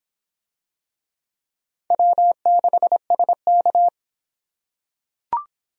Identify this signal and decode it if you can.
Morse code, heard as a keyed tone, about 25 wpm, 700 Hz. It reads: W6SK